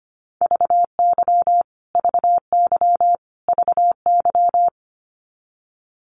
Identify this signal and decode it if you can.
Morse code, heard as a keyed tone, about 25 wpm, 700 Hz. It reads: VY VY VY
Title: Abbreviations Repeated 3x - 25wpm